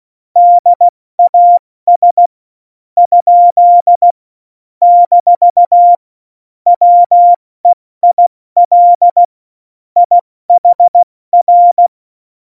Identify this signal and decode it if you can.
Morse code, heard as a keyed tone, about 16 wpm, 700 Hz. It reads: DAS ? - WEIL IHR
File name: Morsekurs / Witze / Witz 25. Dezember, 70 F80